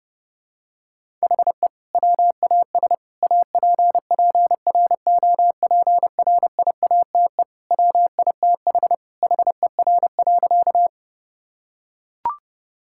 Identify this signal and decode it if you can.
Morse code, heard as a keyed tone, about 30 wpm, 700 Hz. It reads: HE WAS APPROPRIATE WITH HER.
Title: Sentences from Top 1000 Words - Review Part 5 - 30wpm